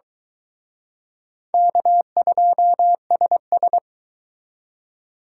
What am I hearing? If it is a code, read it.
Morse code, heard as a keyed tone, about 25 wpm, 700 Hz. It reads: K2SS